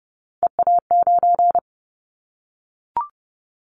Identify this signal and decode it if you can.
Morse code, heard as a keyed tone, about 30 wpm, 700 Hz. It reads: EA9
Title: Callsign Prefixes for Top 100 DXCC - 30wpm